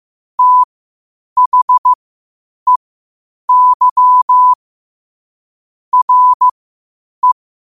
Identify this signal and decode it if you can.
Morse code, heard as a keyed tone, about 15 wpm, spaced as for 8 wpm, 1000 Hz. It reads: THEY RE
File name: qotdfiles / 8wpm